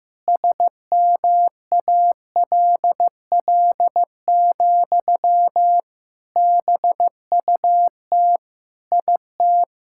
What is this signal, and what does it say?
Morse code, heard as a keyed tone, about 15 wpm, 700 Hz. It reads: SMALL, BUT IT
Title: Sentences from Top 1000 Words - Review Part 6 - 15wpm